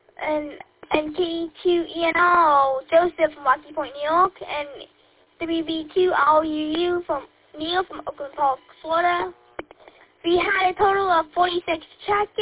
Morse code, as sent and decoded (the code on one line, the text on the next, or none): none